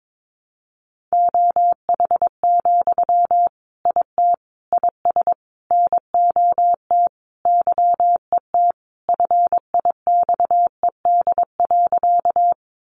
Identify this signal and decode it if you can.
Morse code, heard as a keyed tone, about 22 wpm, 700 Hz. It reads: OH, IT IS NOT YET FIXED.
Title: Sentences from Top 1100 Words 22wpm